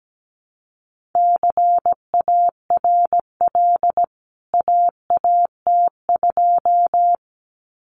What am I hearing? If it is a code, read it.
Morse code, heard as a keyed tone, about 17 wpm, 700 Hz. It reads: CARL AAT2
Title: CWT Exchange 17wpm